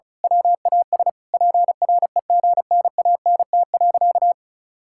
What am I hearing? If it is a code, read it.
Morse code, heard as a keyed tone, about 36 wpm, 700 Hz. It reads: WAS PREGNANT.
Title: Sentences from Top 1000 Words - Review Part 6 - 35wpm